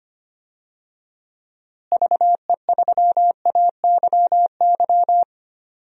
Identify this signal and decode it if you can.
Morse code, heard as a keyed tone, about 25 wpm, 700 Hz. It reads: VE3AYY
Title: DXCC-100 Call Signs - 6 Characters - Repeated 3x - 25wpm